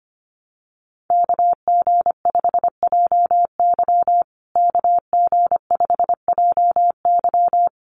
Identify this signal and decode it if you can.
Morse code, heard as a keyed tone, about 25 wpm, 700 Hz. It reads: KG5JY KG5JY
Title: USA Call Signs Repeated 3x 25wpm